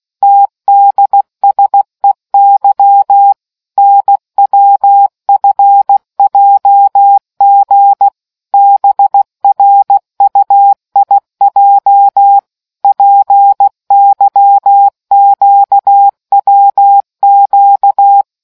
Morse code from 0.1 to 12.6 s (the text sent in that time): TDSEY NWFJG BRUIJ